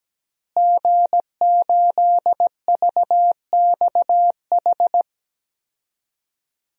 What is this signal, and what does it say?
Morse code, heard as a keyed tone, about 17 wpm, 700 Hz. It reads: G8VXH
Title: DXCC-100 Call Signs - 5 Characters - Repeated 2x - 17wpm